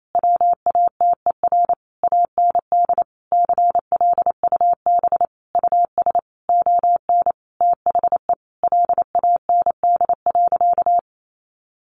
Morse code, heard as a keyed tone, about 28 wpm, 700 Hz. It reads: WATER AND CLUB US ON THE LAND.
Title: Sentences from Top 1200 Words 28wpm